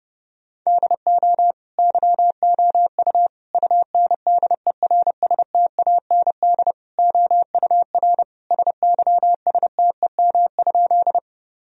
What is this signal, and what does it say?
Morse code, heard as a keyed tone, about 30 wpm, 700 Hz. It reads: DO YOU UNDERSTAND OUR SYSTEM?
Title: Sentences from Top 1000 Words - Review Part 3 - 30wpm